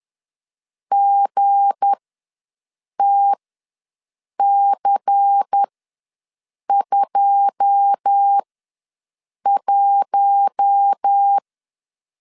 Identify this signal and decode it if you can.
Morse code, heard as a keyed tone, about 10 wpm, spaced as for 6 wpm, 800 Hz. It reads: GTC21
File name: LEZIONI / 9 2 ws 10wpm 4130